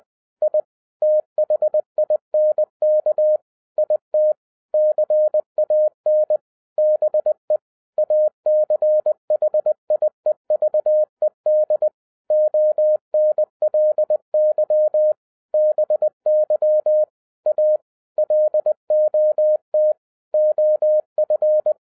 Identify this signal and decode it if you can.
Morse code, heard as a keyed tone, about 20 wpm, 610 Hz. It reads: I THINK IT CAN BE ACHIEVED ONLY BY A LOT OF